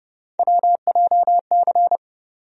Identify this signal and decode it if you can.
Morse code, heard as a keyed tone, about 31 wpm, 700 Hz. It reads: WJC
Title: USA Call Sign Postfixes - 3 Letters 30wpm